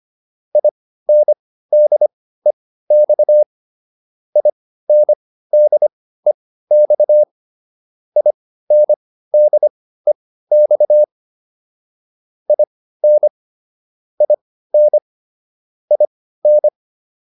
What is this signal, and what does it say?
Morse code, heard as a keyed tone, about 25 wpm, spaced as for 15 wpm, 600 Hz. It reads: INDEX INDEX INDEX IN IN IN